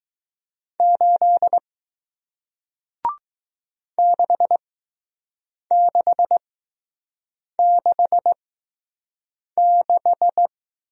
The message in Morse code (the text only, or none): 8 6666